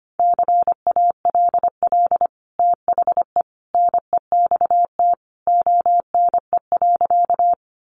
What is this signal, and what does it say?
Morse code, heard as a keyed tone, about 25 wpm, 700 Hz. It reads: CALL THE NEXT ONE.
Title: Sentences from Top 300 Words 25wpm